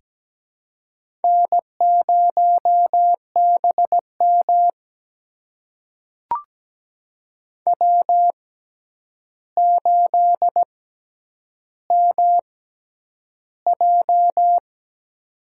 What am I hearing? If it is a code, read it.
Morse code, heard as a keyed tone, about 17 wpm, spaced as for 9 wpm, 700 Hz. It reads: N0BM W8MJ
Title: USA Call Signs - 6x Character Spacing - 17wpm